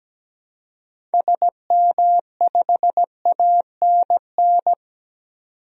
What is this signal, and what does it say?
Morse code, heard as a keyed tone, about 17 wpm, 700 Hz. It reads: SM5ANN
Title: DXCC-100 Call Signs - 6 Characters - 17wpm